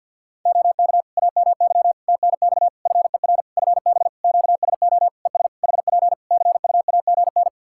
Morse code, heard as a keyed tone, about 49 wpm, 700 Hz. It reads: OK AMY TNX FER FB QSO ES HP CUAGN
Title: Practice QSO - Exchange 4 - Ending - 50wpm